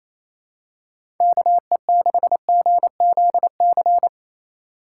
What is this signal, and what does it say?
Morse code, heard as a keyed tone, about 28 wpm, 700 Hz. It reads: KE6GZC
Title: USA Call Signs - 6 Characters - 4x Character Spacing - 28wpm